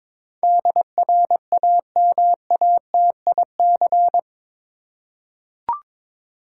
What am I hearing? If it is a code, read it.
Morse code, heard as a keyed tone, about 22 wpm, 700 Hz. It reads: DRAMATIC
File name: practice / 22wpm / Eight Letter Words - 22wpm